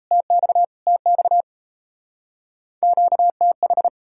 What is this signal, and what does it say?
Morse code, heard as a keyed tone, about 35 wpm, 700 Hz. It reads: TX TX QTH
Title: Speed-Racing Common QSO Phrases - 28wpm